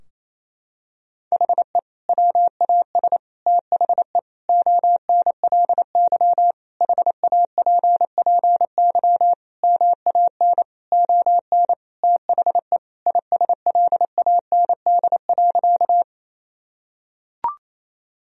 Morse code, heard as a keyed tone, about 28 wpm, 700 Hz. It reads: HE WAS THE ONLY HAPPY MAN ON THE ISLAND.
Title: Sentences from Top 1000 Words - Review Part 3 - 28wpm